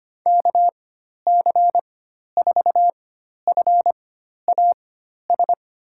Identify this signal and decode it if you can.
Morse code, heard as a keyed tone, about 25 wpm, spaced as for 12 wpm, 700 Hz. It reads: KC4FAS
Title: USA Call Signs - 6 Characters - 4x Character Spacing - 25wpm